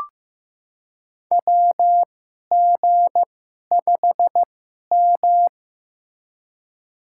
Morse code, heard as a keyed tone, about 15 wpm, spaced as for 11 wpm, 700 Hz. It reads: WG5M
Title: USA Call Signs - 2x Character Spacing - 15wpm